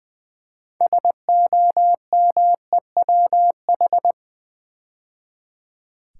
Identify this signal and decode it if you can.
Morse code, heard as a keyed tone, about 20 wpm, 700 Hz. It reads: SOMEWH
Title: Build A Word - 3 Letters At A Time - 9-Letter Words - 20wpm